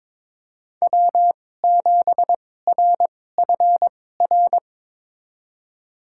Morse code, heard as a keyed tone, about 22 wpm, spaced as for 16 wpm, 700 Hz. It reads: W7RFR
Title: USA Call Signs - 5 Characters - 2x Character Spacing - 22wpm